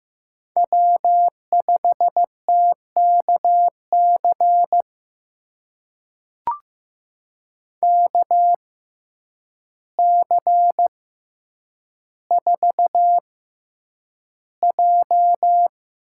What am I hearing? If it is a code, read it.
Morse code, heard as a keyed tone, about 15 wpm, spaced as for 8 wpm, 700 Hz. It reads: W5TKC KC4J
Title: USA Call Signs - 5 Characters - 6x Character Spacing - 15wpm